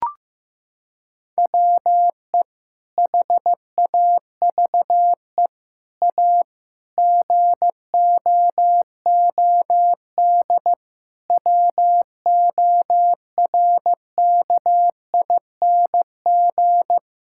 Morse code, heard as a keyed tone, about 15 wpm, 700 Hz. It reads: WE HAVE A GOOD WORKING
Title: Sentences from Top 1000 Words - Review Part 5 - 15wpm